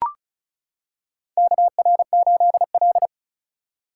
Morse code, heard as a keyed tone, about 36 wpm, 700 Hz. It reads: KR8L
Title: K1USN SST QSO Practice - 35wpm